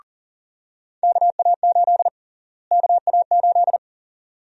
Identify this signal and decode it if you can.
Morse code, heard as a keyed tone, about 37 wpm, 700 Hz. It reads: KA8 KA8
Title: USA Call Sign Prefixes - Repeated 2x - 40wpm